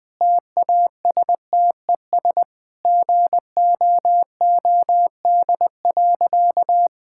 Morse code, heard as a keyed tone, about 20 wpm, 700 Hz. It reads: TASTES GOOD.
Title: Easy Sentences with Hints - 20wpm